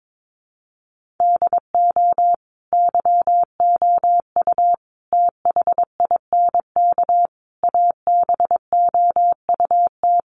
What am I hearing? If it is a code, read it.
Morse code, heard as a keyed tone, about 22 wpm, 700 Hz. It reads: DO YOU THINK ABOUT
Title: Sets of 5 Words Encore2 22wpm